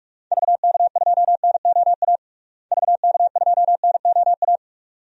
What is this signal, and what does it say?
Morse code, heard as a keyed tone, about 46 wpm, 700 Hz. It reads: VK2NOA VK2NOA
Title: DXCC-100 Call Signs - 6 Characters - Repeated 2x - 45wpm